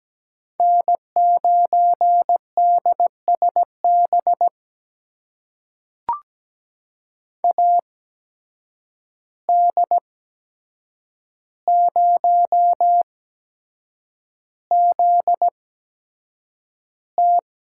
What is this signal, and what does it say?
Morse code, heard as a keyed tone, about 17 wpm, spaced as for 7 wpm, 700 Hz. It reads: N9DSB AD0ZT
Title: USA Call Signs - 5 Characters - 8x Character Spacing - 17wpm